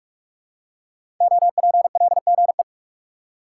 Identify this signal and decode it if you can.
Morse code, heard as a keyed tone, about 44 wpm, 700 Hz. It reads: OPLGE
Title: Code Groups - 5 Letters - 45wpm